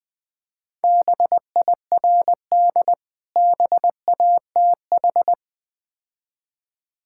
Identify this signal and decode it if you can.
Morse code, heard as a keyed tone, about 20 wpm, 690 Hz. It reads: BIRD BATH
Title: Difficult Words with Hints - 20wpm